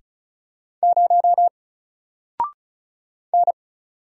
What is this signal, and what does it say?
Morse code, heard as a keyed tone, about 36 wpm, 700 Hz. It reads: 0 N
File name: Single-Letter-Number-35wpm